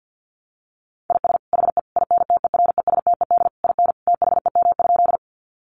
Morse code, heard as a keyed tone, about 50 wpm, 700 Hz. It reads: IS HE INTERESTED IN THEM?
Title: Sentences from Top 900 Words 50wpm